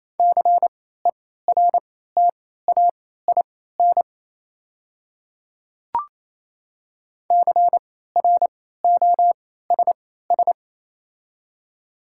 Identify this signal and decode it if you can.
Morse code, heard as a keyed tone, about 27 wpm, spaced as for 16 wpm, 690 Hz. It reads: CERTAIN CROSS
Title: 3x Character Spacing - Top 500 Words - 28wpm